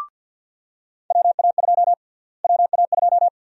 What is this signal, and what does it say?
Morse code, heard as a keyed tone, about 49 wpm, 700 Hz. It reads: WA2 WA2
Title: USA Call Sign Prefixes - Repeated 2x - 50wpm